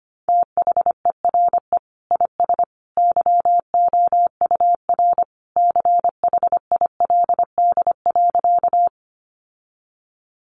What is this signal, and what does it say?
Morse code, heard as a keyed tone, about 25 wpm, 700 Hz. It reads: THERE IS YOUR CHILD.
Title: Sentences from Top 800 Words 25wpm